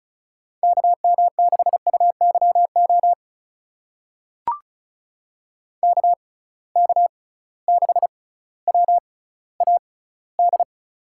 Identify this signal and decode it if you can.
Morse code, heard as a keyed tone, about 36 wpm, spaced as for 18 wpm, 690 Hz. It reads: KM6UYO KK6WAD
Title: USA Call Signs - 6 Characters - 6x Character Spacing - 35wpm